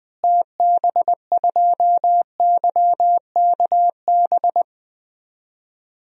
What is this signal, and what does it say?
Morse code, heard as a keyed tone, about 20 wpm, 700 Hz. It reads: TB2YKB
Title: DXCC-100 Call Signs - 6 Characters - Repeated 2x - 20wpm